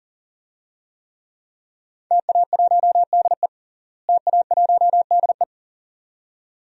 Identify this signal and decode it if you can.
Morse code, heard as a keyed tone, about 40 wpm, 700 Hz. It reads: TA1DE TA1DE
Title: DXCC-100 Call Signs - 5 Characters - Repeated 2x - 40wpm